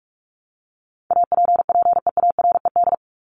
Morse code, heard as a keyed tone, about 44 wpm, 700 Hz. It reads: APPEARED
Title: ED Suffix - Top 100 Words with Base Words 45wpm